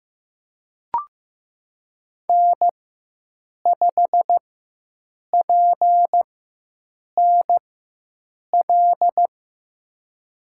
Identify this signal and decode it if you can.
Morse code, heard as a keyed tone, about 15 wpm, spaced as for 7 wpm, 700 Hz. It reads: N5PNL
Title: USA Call Signs - 4x Character Spacing - 15wpm